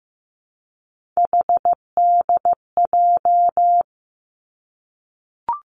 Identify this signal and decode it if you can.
Morse code, heard as a keyed tone, about 15 wpm, 700 Hz. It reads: HDJ
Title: Three Challenging Letter-Number Combinations 15wpm